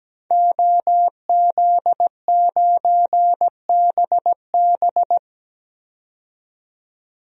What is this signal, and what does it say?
Morse code, heard as a keyed tone, about 17 wpm, 700 Hz. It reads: OZ9BB
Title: DXCC-100 Call Signs - 5 Characters - 17wpm